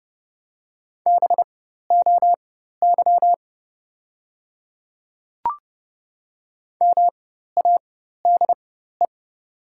Morse code, heard as a keyed tone, about 31 wpm, spaced as for 14 wpm, 700 Hz. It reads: BOY MADE
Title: 4x Character Spacing - Top 200 Words - 30wpm